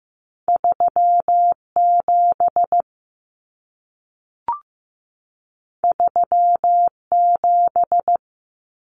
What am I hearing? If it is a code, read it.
Morse code, heard as a keyed tone, about 15 wpm, 700 Hz. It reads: 3 7 3 7
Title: Numbers - 2-Digit 15wpm